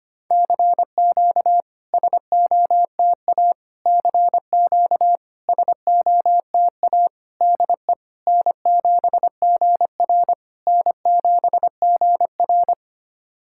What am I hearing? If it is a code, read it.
Morse code, heard as a keyed tone, about 25 wpm, 690 Hz. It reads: CQ SOTA CQ SOTA DE N7GR N7GR
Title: SOTA QSO Practice 25wpm